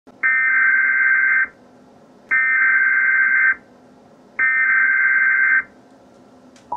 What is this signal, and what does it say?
Emergency Alert System (SAME) tones (header bursts), heard as harsh buzzing bursts. It is heard from 0.1 to 5.7 s.
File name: ORIGINAL-Emergency-Alert-System-Tornado-Warning-for-Knoxville-TN-March-2-2012